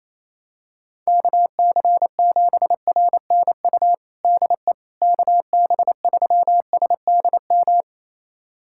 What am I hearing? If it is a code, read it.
Morse code, heard as a keyed tone, about 28 wpm, 700 Hz. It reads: KC7RNU DE KB3SDM
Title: USA Callsign DE Callsign - 6 Characters - 28wpm